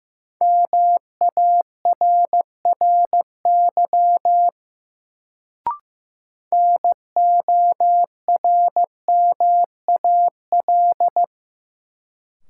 Morse code, heard as a keyed tone, about 15 wpm, 700 Hz. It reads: MARRY NORMAL
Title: Sentences from Top 1200 Words - New Word Drill - 15wpm